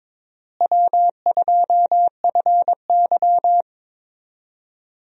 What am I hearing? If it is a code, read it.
Morse code, heard as a keyed tone, about 22 wpm, 700 Hz. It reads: W2FY
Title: USA Call Signs - 4 Characters - Repeated 3x - 22wpm